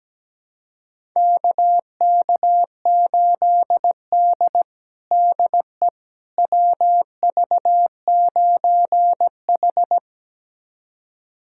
Morse code, heard as a keyed tone, about 17 wpm, 700 Hz. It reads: KK8D DE WV9H
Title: USA Callsign DE Callsign - 4 Characters - 17wpm